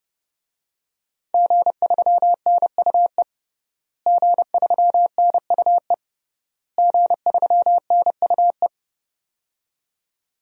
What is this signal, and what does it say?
Morse code, heard as a keyed tone, about 30 wpm, 700 Hz. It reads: G3NUE G3NUE G3NUE
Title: DXCC-100 Call Signs - 5 Characters - Repeated 3x - 30wpm